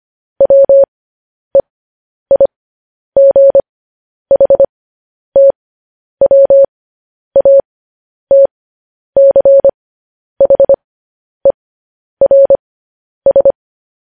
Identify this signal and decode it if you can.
Morse code, heard as a keyed tone, about 25 wpm, spaced as for 10 wpm, 550 Hz. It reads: WEIGHTWATCHERS
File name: podcast / News headlines-1x-25x10